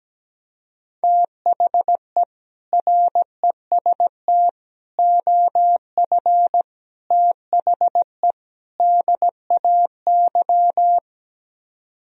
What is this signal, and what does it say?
Morse code, heard as a keyed tone, about 17 wpm, 700 Hz. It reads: THE REST OF THE DAY
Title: Speed-Racing - Sets of 5 Words 17wpm